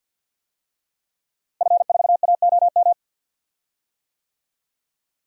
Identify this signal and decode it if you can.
Morse code, heard as a keyed tone, about 51 wpm, 700 Hz. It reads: F4AOM
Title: DXCC-100 Call Signs - 5 Characters - Repeated 2x - 50wpm